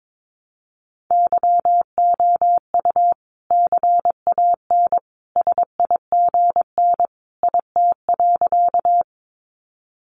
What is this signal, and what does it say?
Morse code, heard as a keyed tone, about 22 wpm, 700 Hz. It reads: YOU CAN SIGN IT.
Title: Sentences from Top 1000 Words 22wpm